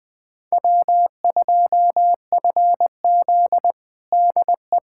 Morse code, heard as a keyed tone, about 20 wpm, 700 Hz. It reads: W2FZ DE
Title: USA Callsign DE Callsign 20wpm